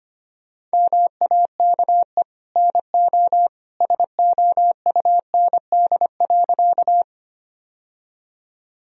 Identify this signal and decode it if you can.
Morse code, heard as a keyed tone, about 25 wpm, 700 Hz. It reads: MAKE NO SOUND.
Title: Sentences from Top 1000 Words - Review Part 1 - 25wpm